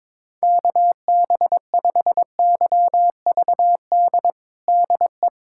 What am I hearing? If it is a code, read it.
Morse code, heard as a keyed tone, about 21 wpm, 700 Hz. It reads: KB5YVD DE
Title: USA Callsign DE Callsign - 6 Characters - 22wpm